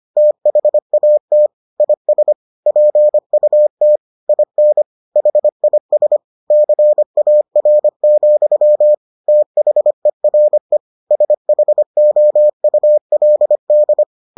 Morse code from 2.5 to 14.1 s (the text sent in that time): PUT IN HIS CAR, THERE SHOULD